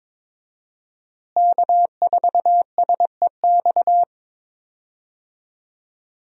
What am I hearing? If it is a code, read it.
Morse code, heard as a keyed tone, about 22 wpm, 700 Hz. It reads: K4SEX
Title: USA Call Signs - 5 Characters - Repeated 3x - 22wpm